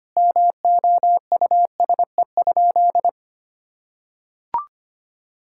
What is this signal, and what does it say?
Morse code, heard as a keyed tone, about 25 wpm, 700 Hz. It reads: MOUSE?
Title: Sentences from Top 1000 Words - Review Part 5 - 25wpm